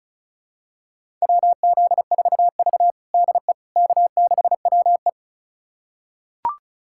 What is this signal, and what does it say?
Morse code, heard as a keyed tone, about 36 wpm, 700 Hz. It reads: WZ4V DE K6WE
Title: USA Callsign DE Callsign - 4 Characters - 35wpm